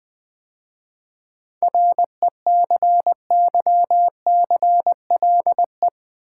Morse code, heard as a keyed tone, about 20 wpm, 690 Hz. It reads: RECYCLE
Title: 7 Letter Words - First 3 Letters Given - 20wpm